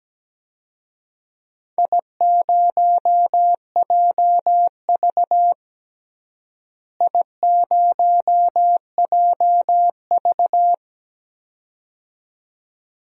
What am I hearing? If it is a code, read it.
Morse code, heard as a keyed tone, about 17 wpm, 700 Hz. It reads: I0JV I0JV
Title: DXCC-100 Call Signs - 4 Characters - Repeated 2x - 17wpm